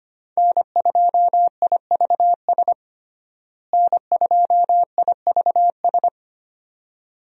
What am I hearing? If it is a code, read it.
Morse code, heard as a keyed tone, about 25 wpm, 700 Hz. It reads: N2IVS N2IVS
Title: USA Call Signs - 5 Characters - Repeated 2x - 25wpm